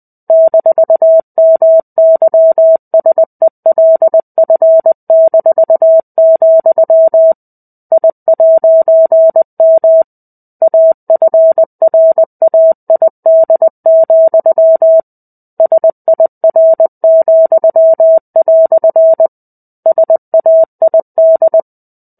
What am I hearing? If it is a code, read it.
Morse code, heard as a keyed tone, about 20 wpm, 650 Hz. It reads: -MYSELF-, I'M AFRAID, SIR," SAID
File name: 05-Chapter-05 20 WPM